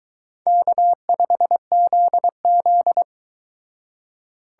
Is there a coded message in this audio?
Yes: Morse code from 0.5 to 3.0 s, heard as a keyed tone, about 22 wpm, 700 Hz. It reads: K5ZZ